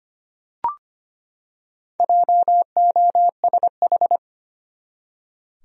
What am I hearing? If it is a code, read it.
Morse code, heard as a keyed tone, about 25 wpm, 700 Hz. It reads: JOSH
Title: USA Common Names - Max 5 Letters - 25wpm